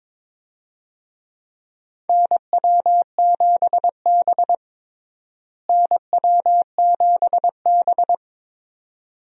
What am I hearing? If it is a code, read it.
Morse code, heard as a keyed tone, about 22 wpm, 700 Hz. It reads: NW7B NW7B
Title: USA Call Signs - 4 Characters - Repeated 2x - 22wpm